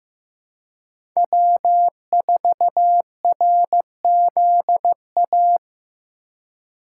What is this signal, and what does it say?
Morse code, heard as a keyed tone, about 15 wpm, 700 Hz. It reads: W4RZA
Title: USA Call Signs - 5 Characters - 4x Character Spacing - 15wpm